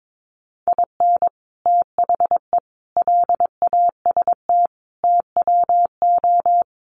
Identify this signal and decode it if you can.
Morse code, heard as a keyed tone, about 22 wpm, 700 Hz. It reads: IN THE LAST TWO
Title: Sets of 5 Words 22wpm